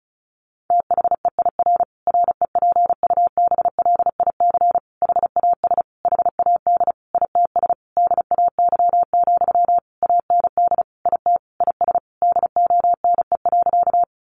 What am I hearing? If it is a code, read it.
Morse code, heard as a keyed tone, about 34 wpm, 700 Hz. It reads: THEIR REPUBLIC HAS HAD ITS DAY, AND IT IS DONE.
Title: Sentences from Top 1500 Words 35wpm